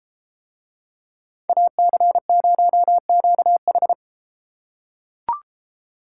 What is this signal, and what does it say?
Morse code, heard as a keyed tone, about 33 wpm, 700 Hz. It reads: AC0QH